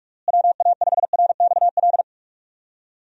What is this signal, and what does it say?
Morse code, heard as a keyed tone, about 46 wpm, 700 Hz. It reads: WA5RXL
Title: USA Call Signs - 6 Characters - Repeated 3x - 45wpm